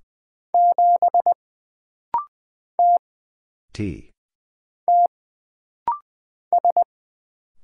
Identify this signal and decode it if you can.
Morse code, heard as a keyed tone, about 20 wpm, 700 Hz. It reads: 7 T T S
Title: Single Letter-Number Alt - Rapid-Fire 20wpm